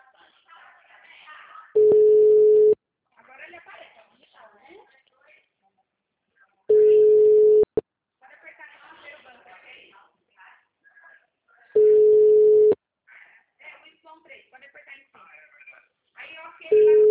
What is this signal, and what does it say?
Ringback tone (Europe, 425 Hz), with speech.
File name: out- -2009- - - .226